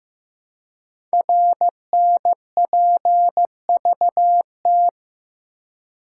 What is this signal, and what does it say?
Morse code, heard as a keyed tone, about 15 wpm, 700 Hz. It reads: RNPVT
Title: Code Groups - 5 Letters - 15wpm